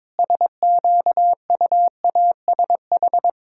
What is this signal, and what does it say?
Morse code, heard as a keyed tone, about 22 wpm, 700 Hz. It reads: SQUASH
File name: Difficult Words with Hints - 22wpm